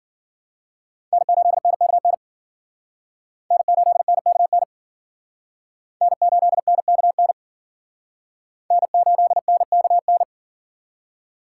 Morse code, heard as a keyed tone, about 51 wpm, 700 Hz. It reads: N8NKN N8NKN N8NKN N8NKN